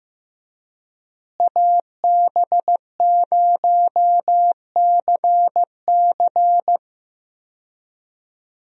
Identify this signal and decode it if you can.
Morse code, heard as a keyed tone, about 16 wpm, 700 Hz. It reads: AB0CC
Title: USA Call Signs - 5 Characters - Repeated 2x - 15wpm